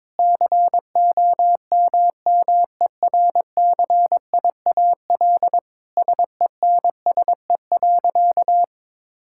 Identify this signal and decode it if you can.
Morse code, heard as a keyed tone, about 22 wpm, 700 Hz. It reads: COMMERCIAL SENSE.
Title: Sentences from Top 1000 Words - Review Part 5 - 22wpm